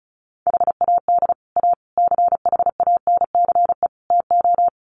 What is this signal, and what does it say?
Morse code, heard as a keyed tone, about 34 wpm, 700 Hz. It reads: HAD A CHANCE TO
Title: Sets of 4 Words 35wpm